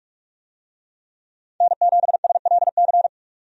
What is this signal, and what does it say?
Morse code, heard as a keyed tone, about 43 wpm, 700 Hz. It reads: N7SLC
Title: USA Call Signs - 5 Characters - 45wpm